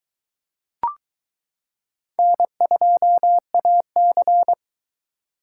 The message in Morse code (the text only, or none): N2AC